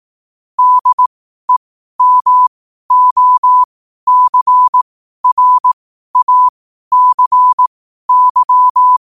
Morse code, heard as a keyed tone, about 18 wpm, spaced as for 12 wpm, 1000 Hz. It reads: DEMOCRACY
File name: qotdfiles / 12wpm